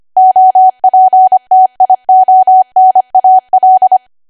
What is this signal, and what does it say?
Morse code, heard as a keyed tone, about 25 wpm, 750 Hz. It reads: OPTIONAL
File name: Archive / 25 WPM / 25WPM